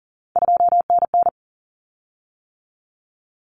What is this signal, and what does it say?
Morse code, heard as a keyed tone, about 37 wpm, 700 Hz. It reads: 2NN
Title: Common Cut Numbers - 3-Digit 40wpm